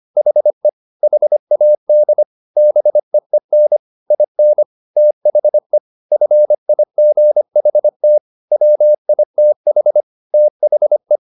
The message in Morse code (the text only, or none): HE HAD BEEN IN THE FIGHT WITH THE